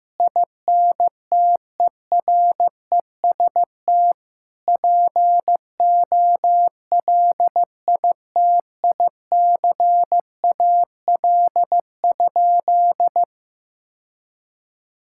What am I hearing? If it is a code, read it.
Morse code, heard as a keyed tone, about 15 wpm, 700 Hz. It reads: INTEREST POLITICAL?
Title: Sentences from Top 1000 Words - Review Part 5 - 15wpm